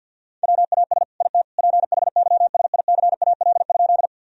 Morse code, heard as a keyed tone, about 50 wpm, 700 Hz. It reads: WAS IT PHYSICAL?
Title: Sentences from Top 1000 Words - Review Part 4 - 50wpm